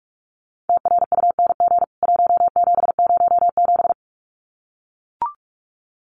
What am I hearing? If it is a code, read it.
Morse code, heard as a keyed tone, about 44 wpm, 700 Hz. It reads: TRUNG 1707
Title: CWT Exchange 45wpm